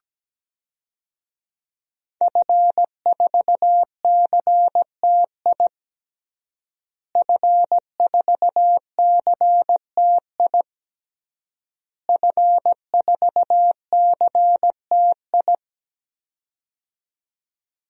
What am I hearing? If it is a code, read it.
Morse code, heard as a keyed tone, about 17 wpm, 700 Hz. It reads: F4CTI F4CTI F4CTI
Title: DXCC-100 Call Signs - 5 Characters - Repeated 3x - 17wpm